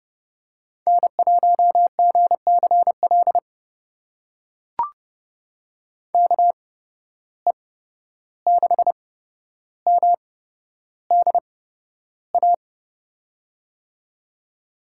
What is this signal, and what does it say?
Morse code, heard as a keyed tone, about 30 wpm, spaced as for 12 wpm, 700 Hz. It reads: N1GCL KE6MDA
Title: USA Call Signs - 8x Character Spacing - 30wpm